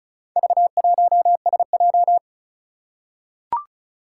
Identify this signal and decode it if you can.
Morse code, heard as a keyed tone, about 36 wpm, 700 Hz. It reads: V1SJ
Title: Four Challenging Letter-Number Combinations 35wpm